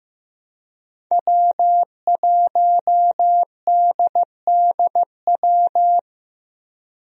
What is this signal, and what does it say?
Morse code, heard as a keyed tone, about 15 wpm, 700 Hz. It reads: W1DDW
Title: USA Call Signs - 5 Characters - Repeated 2x - 15wpm